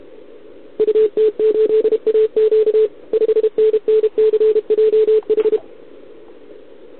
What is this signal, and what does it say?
Morse code, heard as a keyed tone, about 32 wpm, 440 Hz. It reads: UT8AQ 5NNCJH
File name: ARCHIV / OKDX2020 / OK1RIb